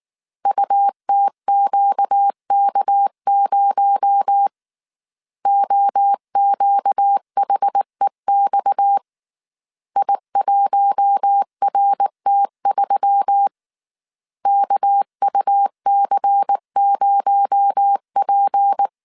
Morse code, heard as a keyed tone, about 19 wpm, 800 Hz. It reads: UTQK0 OQHEX I1RT3 KUC0P
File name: CW / DET / 1 6 1500C S A 18wpm 2255